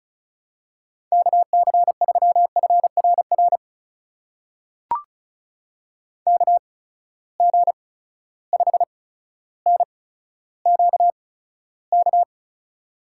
Morse code, heard as a keyed tone, about 36 wpm, spaced as for 15 wpm, 700 Hz. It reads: KC3FRR KG5NQK
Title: USA Call Signs - 6 Characters - 8x Character Spacing - 35wpm